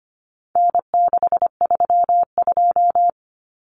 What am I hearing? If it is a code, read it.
Morse code, heard as a keyed tone, about 25 wpm, 700 Hz. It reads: N632
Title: Common Cut Numbers - 4-Digit 25wpm